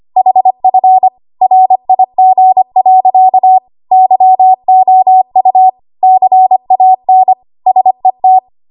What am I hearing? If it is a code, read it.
Morse code, heard as a keyed tone, about 25 wpm, 750 Hz. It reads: HF RIG. YOU CAN SET
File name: Archive / 25 WPM / 25WPM